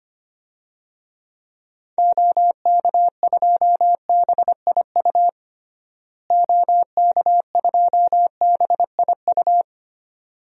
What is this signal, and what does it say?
Morse code, heard as a keyed tone, about 25 wpm, 700 Hz. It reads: OK2BIU OK2BIU
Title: DXCC-100 Call Signs - 6 Characters - Repeated 2x - 25wpm